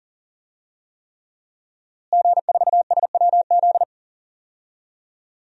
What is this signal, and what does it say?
Morse code, heard as a keyed tone, about 40 wpm, 700 Hz. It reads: G4SWZ
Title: DXCC-100 Call Signs - 5 Characters - Repeated 3x - 40wpm